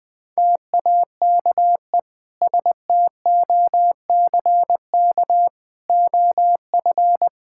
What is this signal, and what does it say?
Morse code, heard as a keyed tone, about 20 wpm, 700 Hz. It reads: TAKE STOCK OF
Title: Sentences from Top 1000 Words - Review Part 5 - 20wpm